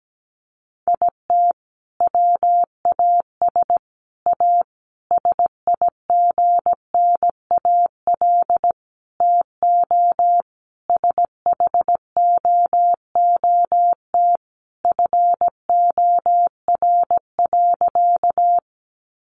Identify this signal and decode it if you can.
Morse code, heard as a keyed tone, about 17 wpm, 700 Hz. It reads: IT WAS A SIGNAL TO SHOOT FOR.
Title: Sentences from Top 1400 Words 17wpm